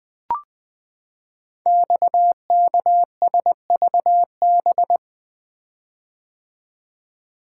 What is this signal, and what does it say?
Morse code, heard as a keyed tone, about 20 wpm, 700 Hz. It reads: XKSVB
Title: Code Groups - 5 Letters - 20wpm